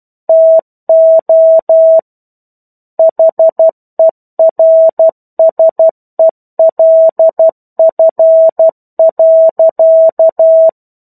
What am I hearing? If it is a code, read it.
Morse code, heard as a keyed tone, about 12 wpm, 650 Hz. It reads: TO HERSELF.
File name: 01-Chapter-01 12 WPM